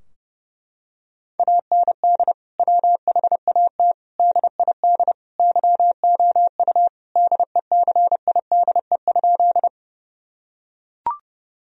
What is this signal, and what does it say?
Morse code, heard as a keyed tone, about 30 wpm, 700 Hz. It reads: AND WHAT DID YOU DECIDE?
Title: Sentences from Top 1000 Words - Review Part 3 - 30wpm